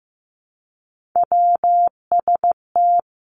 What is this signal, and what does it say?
Morse code, heard as a keyed tone, about 15 wpm, 700 Hz. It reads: WST
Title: Three Challenging Letter-Number Combinations 15wpm